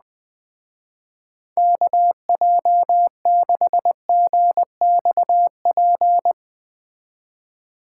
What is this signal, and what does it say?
Morse code, heard as a keyed tone, about 20 wpm, 700 Hz. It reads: KJ6GXP